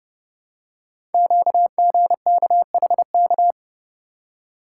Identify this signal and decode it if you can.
Morse code, heard as a keyed tone, about 30 wpm, 700 Hz. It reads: QGKHK
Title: Code Groups - 5 Letters - 30wpm